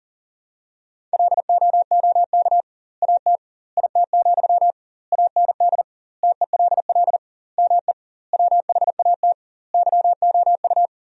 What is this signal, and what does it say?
Morse code, heard as a keyed tone, about 40 wpm, 700 Hz. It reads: LOOK AT IT, AND TELL ME WHAT YOU
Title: 2x Word-Spacing - Sentences from Top 200 Words - 40wpm